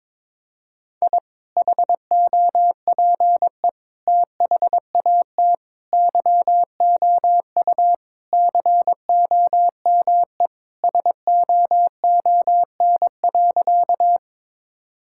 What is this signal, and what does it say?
Morse code, heard as a keyed tone, about 22 wpm, 700 Hz. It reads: I HOPE THAT YOU COME SOON.
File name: practice / 22wpm / Sentences with 2-Word Phrases - 22wpm